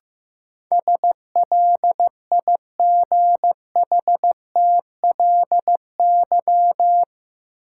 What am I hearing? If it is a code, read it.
Morse code, heard as a keyed tone, about 15 wpm, 700 Hz. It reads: SLIGHTLY
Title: Sentences from Top 1300 Words - New Word Drill Alt - 15wpm